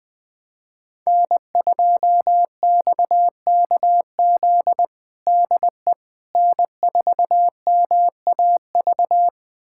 Morse code, heard as a keyed tone, about 20 wpm, 700 Hz. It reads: N2XKZ DE N4MAV
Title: USA Callsign DE Callsign - 5 Characters - 20wpm